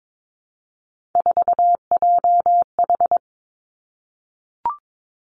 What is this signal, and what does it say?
Morse code, heard as a keyed tone, about 22 wpm, 700 Hz. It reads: 4JH
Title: Three Challenging Letter-Number Combinations 22wpm